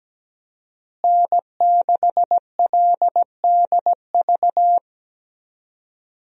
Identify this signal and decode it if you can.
Morse code, heard as a keyed tone, about 17 wpm, 700 Hz. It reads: N6LDV
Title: USA Call Signs - 5 Characters - Repeated 2x - 17wpm